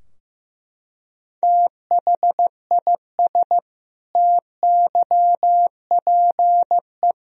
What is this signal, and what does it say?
Morse code, heard as a keyed tone, about 15 wpm, 700 Hz. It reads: THIS TYPE
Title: Sentences from Top 1000 Words - Review Part 3 - 15wpm